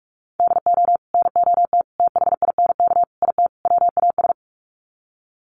Morse code, heard as a keyed tone, about 44 wpm, 700 Hz. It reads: DO NOT THINK IT WAS
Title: Sets of 5 Words Encore 45wpm